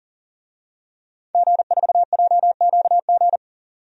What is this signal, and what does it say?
Morse code, heard as a keyed tone, about 37 wpm, 700 Hz. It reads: G4JQG
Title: DXCC-100 Call Signs - 5 Characters - 40wpm